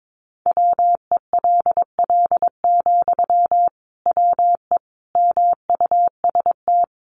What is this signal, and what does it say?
Morse code, heard as a keyed tone, about 22 wpm, 700 Hz. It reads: WELL, WE MUST
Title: Sentences from Top 1400 Words 22wpm